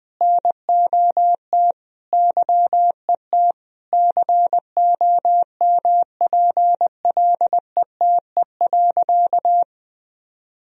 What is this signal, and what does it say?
Morse code, heard as a keyed tone, about 21 wpm, 700 Hz. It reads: NOT YET COMPLETE.
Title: Sentences from Top 1000 Words - Review Part 2 - 20wpm